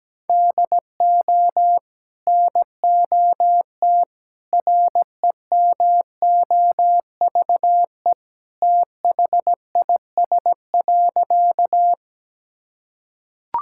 Morse code, heard as a keyed tone, about 17 wpm, 700 Hz. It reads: DO NOT REMOVE THIS.
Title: Sentences from Top 1000 Words - Review Part 5 - 17wpm